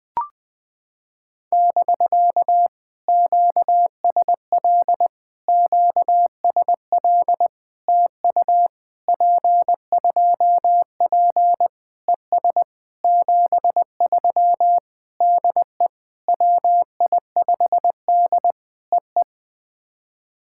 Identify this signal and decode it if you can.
Morse code, heard as a keyed tone, about 19 wpm, 700 Hz. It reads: <BK> QSL QSL TU P2P ES 73 DE WI5D EE
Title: POTA QSO Practice - 20wpm